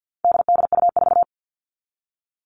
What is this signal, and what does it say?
Morse code, heard as a keyed tone, about 50 wpm, 700 Hz. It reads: DDU4
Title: All Cut Numbers - 4-Digit 50wpm